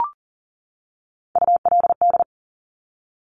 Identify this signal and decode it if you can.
Morse code, heard as a keyed tone, about 38 wpm, 700 Hz. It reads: ULD
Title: Top Three Letter Combinations in Words 40wpm